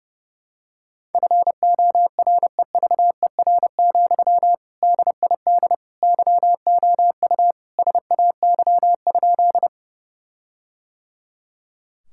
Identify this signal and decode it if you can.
Morse code, heard as a keyed tone, about 30 wpm, 700 Hz. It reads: FOREVER, DID YOU SAY?
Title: Sentences from Top 1600 Words 30wpm